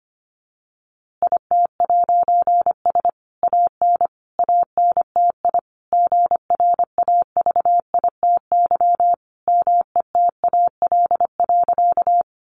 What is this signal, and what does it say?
Morse code, heard as a keyed tone, about 25 wpm, 700 Hz. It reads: IT'S AN ANTI GRAVITY METAL.
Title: A Sound Of Thunder 25wpm